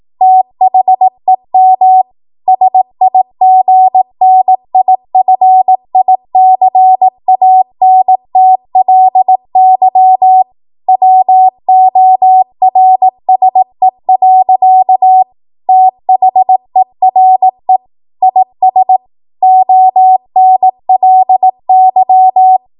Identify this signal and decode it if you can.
Morse code, heard as a keyed tone, about 18 wpm, 750 Hz. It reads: THEM SIGNIFICANTLY WORSE. THERE IS ONLY
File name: Morse / 18WPM